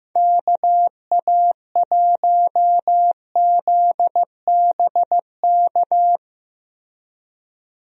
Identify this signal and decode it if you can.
Morse code, heard as a keyed tone, about 15 wpm, 700 Hz. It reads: KA1ZBK
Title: USA Call Signs - 6 Characters - Repeated 3x - 15wpm